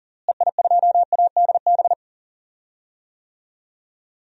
Morse code, heard as a keyed tone, about 40 wpm, 700 Hz. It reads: EI2ADB
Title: DXCC-100 Call Signs - 6 Characters - Repeated 3x - 40wpm